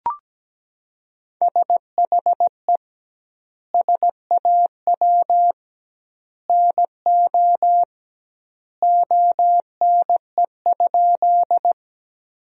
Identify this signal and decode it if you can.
Morse code, heard as a keyed tone, about 17 wpm, 700 Hz. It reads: SHE SAW NO ONE?
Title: 2x Word-Spacing - Sentences from Top 300 Words - 17wpm